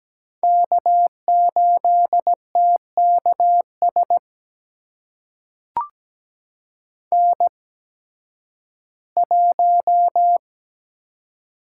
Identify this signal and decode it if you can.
Morse code, heard as a keyed tone, about 17 wpm, spaced as for 11 wpm, 700 Hz. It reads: K8TKS N1
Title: USA Call Signs - 5 Characters - 8x Character Spacing - 17wpm